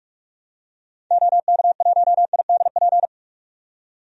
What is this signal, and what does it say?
Morse code, heard as a keyed tone, about 46 wpm, 700 Hz. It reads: OK1IDP
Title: DXCC-100 Call Signs - 6 Characters - 45wpm